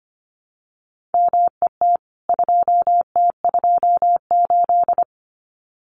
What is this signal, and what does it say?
Morse code, heard as a keyed tone, about 25 wpm, 700 Hz. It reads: MET 2T28
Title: CWT Exchange 25wpm